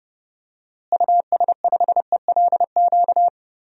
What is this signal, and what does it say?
Morse code, heard as a keyed tone, about 30 wpm, 700 Hz. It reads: US5ELQ
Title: DXCC-100 Call Signs - 6 Characters - Repeated 2x - 30wpm